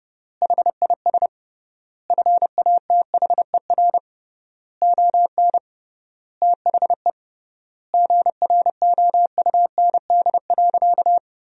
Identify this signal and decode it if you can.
Morse code, heard as a keyed tone, about 30 wpm, 700 Hz. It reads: HIS FATHER ON THE GROUND.
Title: 3x Word-Spacing - Sentences from Top 500 Words - 30wpm